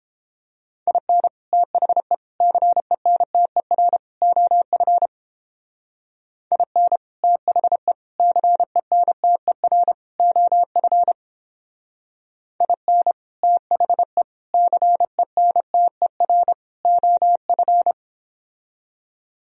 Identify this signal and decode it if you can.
Morse code, heard as a keyed tone, about 29 wpm, 700 Hz. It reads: IN THE CENTER OF IN THE CENTER OF IN THE CENTER OF